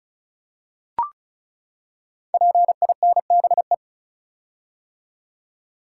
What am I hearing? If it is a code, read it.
Morse code, heard as a keyed tone, about 36 wpm, 690 Hz. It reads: PINBE
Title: Code Groups - 5 Letter:Numbers - 35wpm